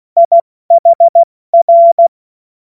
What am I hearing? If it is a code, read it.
Morse code, heard as a keyed tone, about 16 wpm, 680 Hz. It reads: IHR
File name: Morsekurs / Witze / Witz 16 Januar 2024 70 F80